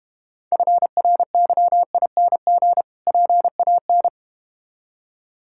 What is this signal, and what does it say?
Morse code, heard as a keyed tone, about 31 wpm, 700 Hz. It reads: FRYING PAN
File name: Difficult Words - Speed-Racing with Hints - 25wpm